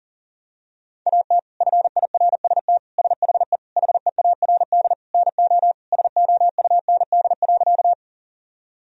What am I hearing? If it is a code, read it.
Morse code, heard as a keyed tone, about 40 wpm, 700 Hz. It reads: AT FIRST SHE HEARD NO SOUND.
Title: Sentences from Top 1000 Words - Review Part 1 - 40wpm